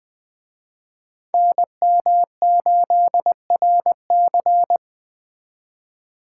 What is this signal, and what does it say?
Morse code, heard as a keyed tone, about 20 wpm, 700 Hz. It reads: NM8RC
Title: Speed Racing - USA 2x2 Call Signs 17wpm